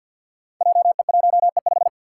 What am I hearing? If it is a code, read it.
Morse code, heard as a keyed tone, about 52 wpm, 700 Hz. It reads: JE1E5
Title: Five Challenging Letter-Number Combinations 50wpm